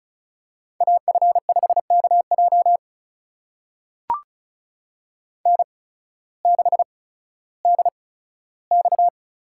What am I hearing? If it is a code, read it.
Morse code, heard as a keyed tone, about 36 wpm, spaced as for 16 wpm, 700 Hz. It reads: AF5KJ N6DX
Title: USA Call Signs - 5 Characters - 8x Character Spacing - 35wpm